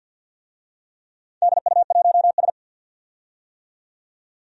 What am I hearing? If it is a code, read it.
Morse code, heard as a keyed tone, about 53 wpm, 700 Hz. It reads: DU1S
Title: DXCC-100 Call Signs - 4 Characters - 50wpm